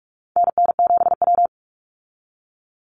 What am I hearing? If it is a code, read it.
Morse code, heard as a keyed tone, about 44 wpm, 700 Hz. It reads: NN7W
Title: Call Signs Repeated 3x - 45wpm